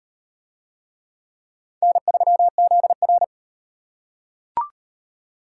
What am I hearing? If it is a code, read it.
Morse code, heard as a keyed tone, about 34 wpm, 700 Hz. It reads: N3ZR